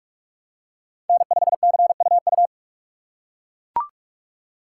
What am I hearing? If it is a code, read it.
Morse code, heard as a keyed tone, about 39 wpm, 700 Hz. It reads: N5CUU